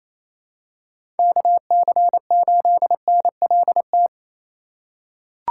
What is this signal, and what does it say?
Morse code, heard as a keyed tone, about 28 wpm, 700 Hz. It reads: KC8NLT
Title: USA Call Signs - 6 Characters - 3x Character Spacing - 28wpm